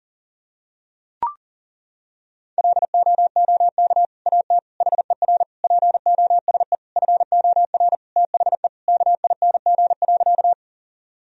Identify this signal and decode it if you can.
Morse code, heard as a keyed tone, about 37 wpm, 700 Hz. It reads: LOOK AT HER POSE FOR THE KING.
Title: Sentences from Top 1000 Words - Review Part 2 - 40wpm